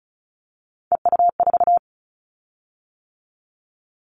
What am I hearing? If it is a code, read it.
Morse code, heard as a keyed tone, about 36 wpm, 700 Hz. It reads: EU4
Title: Callsign Prefixes for Top 50 DXCC - 35wpm